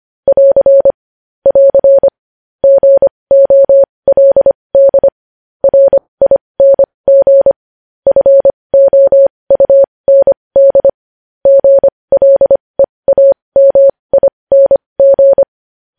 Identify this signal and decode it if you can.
Morse code, heard as a keyed tone, about 25 wpm, spaced as for 20 wpm, 550 Hz. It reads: <AR> <AR> GOLD RING FOUND GLEAMING
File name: podcast / News headlines-1x-25x20